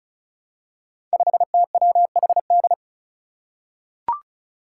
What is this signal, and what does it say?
Morse code, heard as a keyed tone, about 36 wpm, 700 Hz. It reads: 5TWHD
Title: Five Challenging Letter-Number Combinations 35wpm